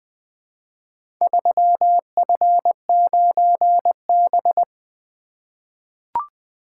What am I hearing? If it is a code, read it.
Morse code, heard as a keyed tone, about 20 wpm, 700 Hz. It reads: 3F9B
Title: Four Letter-Number Combinations - 20wpm